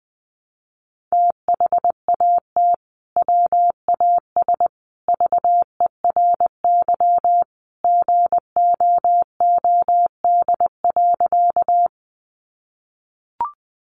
Morse code, heard as a keyed tone, about 20 wpm, 700 Hz. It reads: THAT WAS VERY GOOD.
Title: Sentences from Top 200 Words 20wpm